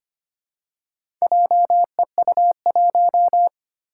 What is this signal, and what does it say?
Morse code, heard as a keyed tone, about 25 wpm, 700 Hz. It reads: JEU1
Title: Four Challenging Letter-Number Combinations 25wpm